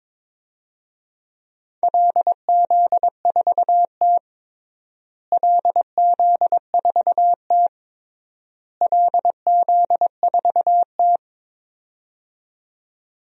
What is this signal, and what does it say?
Morse code, heard as a keyed tone, about 22 wpm, 700 Hz. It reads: LZ4T LZ4T LZ4T
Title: DXCC-100 Call Signs - 4 Characters - Repeated 3x - 22wpm